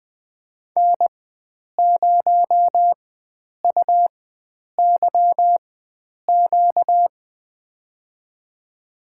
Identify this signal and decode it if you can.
Morse code, heard as a keyed tone, about 20 wpm, spaced as for 9 wpm, 700 Hz. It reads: N0UYQ
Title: USA Call Signs - 5 Characters - 4x Character Spacing - 20wpm